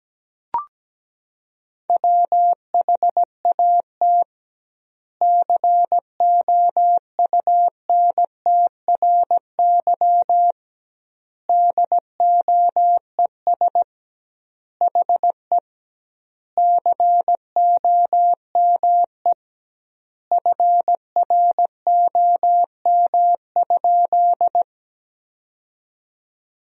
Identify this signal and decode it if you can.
Morse code, heard as a keyed tone, about 17 wpm, 700 Hz. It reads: WHAT COUNTRY DOES HE COME FROM?
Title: 2x Word-Spacing - Sentences from Top 300 Words - 17wpm